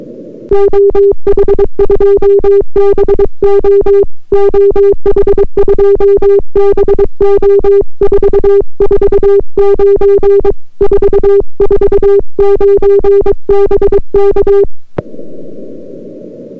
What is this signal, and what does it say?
Morse code, heard as a keyed tone, about 22 wpm, 400 Hz. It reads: OH2BO OH2BO 449 449 BK